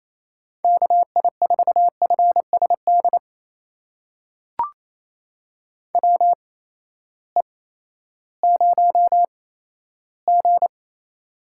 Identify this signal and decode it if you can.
Morse code, heard as a keyed tone, about 28 wpm, spaced as for 14 wpm, 700 Hz. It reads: KI4FSD WE0G
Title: USA Call Signs - 8x Character Spacing - 28wpm